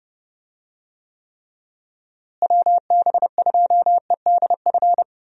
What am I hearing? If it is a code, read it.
Morse code, heard as a keyed tone, about 30 wpm, 700 Hz. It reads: WB2EDF
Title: USA Call Signs - 6 Characters - Repeated 2x - 30wpm